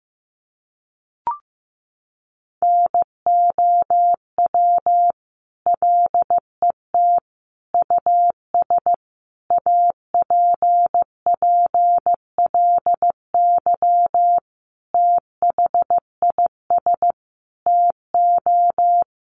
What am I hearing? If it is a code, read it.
Morse code, heard as a keyed tone, about 15 wpm, 700 Hz. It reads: NOW LET US APPLY THIS TO
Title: Sentences from Top 1200 Words 15wpm